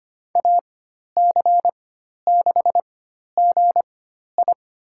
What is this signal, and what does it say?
Morse code, heard as a keyed tone, about 25 wpm, spaced as for 12 wpm, 700 Hz. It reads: AC6GI
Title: USA Call Signs - 5 Characters - 4x Character Spacing - 25wpm